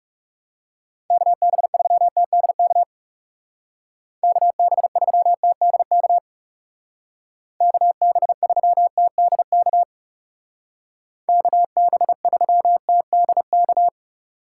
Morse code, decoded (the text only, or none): KB3TDK KB3TDK KB3TDK KB3TDK